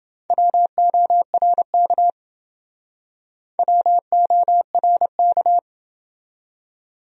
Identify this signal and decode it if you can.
Morse code, heard as a keyed tone, about 28 wpm, 700 Hz. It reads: WORK WORK